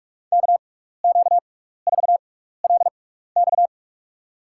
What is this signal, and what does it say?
Morse code, heard as a keyed tone, about 47 wpm, spaced as for 16 wpm, 700 Hz. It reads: KQ4LX
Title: USA Call Signs - 6x Character Spacing - 45wpm